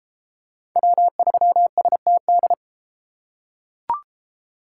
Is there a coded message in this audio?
Yes: Morse code from 0.8 to 2.5 s, heard as a keyed tone, about 32 wpm, 700 Hz. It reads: W3STD